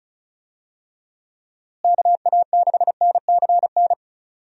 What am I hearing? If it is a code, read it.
Morse code, heard as a keyed tone, about 36 wpm, 700 Hz. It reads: KA6NCN
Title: USA Call Signs - 6 Characters - Repeated 2x - 35wpm